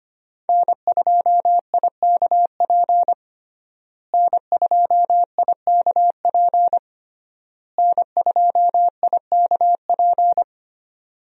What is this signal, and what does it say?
Morse code, heard as a keyed tone, about 25 wpm, 700 Hz. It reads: N2IKP N2IKP N2IKP
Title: USA Call Signs - 5 Characters - Repeated 3x - 25wpm